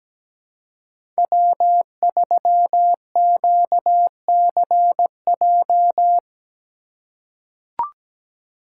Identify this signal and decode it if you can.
Morse code, heard as a keyed tone, about 17 wpm, 700 Hz. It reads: W3QCJ
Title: USA Call Signs - 2x Character Spacing - 17wpm